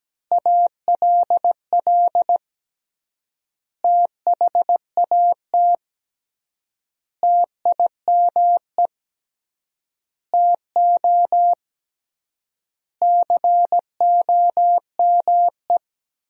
Morse code, heard as a keyed tone, about 17 wpm, 700 Hz. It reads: ALL THAT TIME TO COME
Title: 3x Word-Spacing - Sentences from Top 300 Words - 17wpm